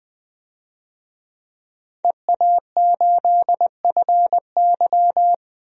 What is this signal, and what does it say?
Morse code, heard as a keyed tone, about 20 wpm, 700 Hz. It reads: EA8FY
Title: DXCC-100 Call Signs - 5 Characters - 20wpm